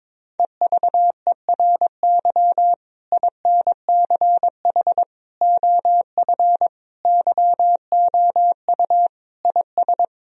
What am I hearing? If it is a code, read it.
Morse code, heard as a keyed tone, about 22 wpm, 700 Hz. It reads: EVERY INCH OF YOU IS
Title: Sentences from Top 1000 Words - Review Part 3 - 22wpm